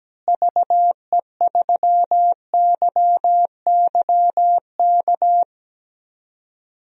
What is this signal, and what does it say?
Morse code, heard as a keyed tone, about 17 wpm, 700 Hz. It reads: VE3YYK
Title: DXCC-100 Call Signs - 6 Characters - Repeated 3x - 17wpm